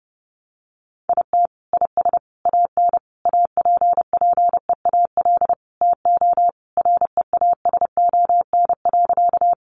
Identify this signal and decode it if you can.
Morse code, heard as a keyed tone, about 30 wpm, 700 Hz. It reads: IT IS AN APPEAL TO REASON.
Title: Sentences from Top 1300 Words 30wpm